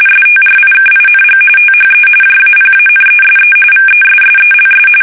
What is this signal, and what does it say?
RTTY, about 45.45 baud. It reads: STRIKING THE JAPANESE VESSELS
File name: fsk850 / battleshipnj